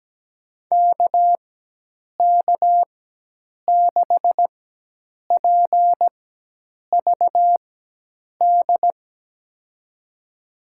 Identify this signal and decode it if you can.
Morse code, heard as a keyed tone, about 17 wpm, spaced as for 8 wpm, 700 Hz. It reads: KK6PVD
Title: USA Call Signs - 6 Characters - 4x Character Spacing - 17wpm